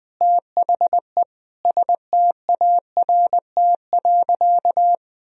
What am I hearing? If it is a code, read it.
Morse code, heard as a keyed tone, about 20 wpm, 700 Hz. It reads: THE START.
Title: Sentences from Top 1000 Words - Review Part 1 - 20wpm